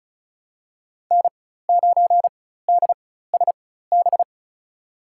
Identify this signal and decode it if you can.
Morse code, heard as a keyed tone, about 34 wpm, spaced as for 16 wpm, 700 Hz. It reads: N9DSB
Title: USA Call Signs - 5 Characters - 4x Character Spacing - 35wpm